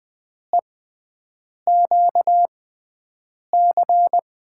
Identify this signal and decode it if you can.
Morse code, heard as a keyed tone, about 21 wpm, spaced as for 7 wpm, 700 Hz. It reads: EQC
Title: USA Call Signs - 5 Characters - 6x Character Spacing - 20wpm